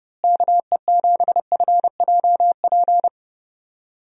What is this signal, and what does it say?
Morse code, heard as a keyed tone, about 30 wpm, 700 Hz. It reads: KE7FJP
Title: USA Call Signs - 6 Characters - Repeated 3x - 30wpm